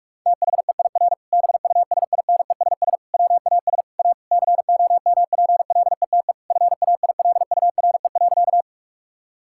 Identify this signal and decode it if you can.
Morse code, heard as a keyed tone, about 41 wpm, 700 Hz. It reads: THEIR BUSINESS WAS A COMPLETE FAILURE.
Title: Sentences from Top 1000 Words - Review Part 6 - 45wpm